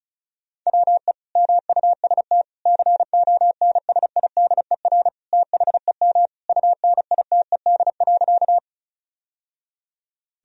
Morse code, heard as a keyed tone, about 36 wpm, 700 Hz. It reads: WE MUST CONSIDER THEM UNITED.
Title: Sentences from Top 1000 Words - Review Part 4 - 35wpm